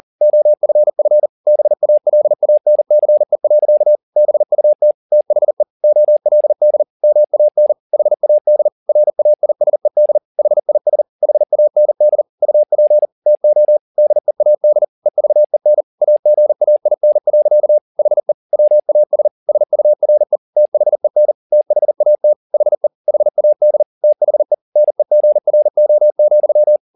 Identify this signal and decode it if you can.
Morse code, heard as a keyed tone, about 40 wpm, 600 Hz. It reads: OFF BALANCE. BUT THE OLD MAN HAD RAISED HIS HAND UP TO DEAD EVEN AGAIN. HE WAS SURE THEN THAT HE HAD THE NEGRO,
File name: the old man and the sea 2 40wpm